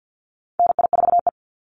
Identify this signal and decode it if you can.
Morse code, heard as a keyed tone, about 49 wpm, 700 Hz. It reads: NI4E
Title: Call Signs Repeated 3x - 50wpm